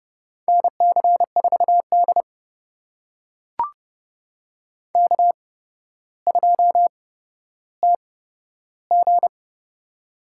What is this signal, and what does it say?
Morse code, heard as a keyed tone, about 30 wpm, spaced as for 13 wpm, 700 Hz. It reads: NC4D K2TG
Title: USA Call Signs - 8x Character Spacing - 30wpm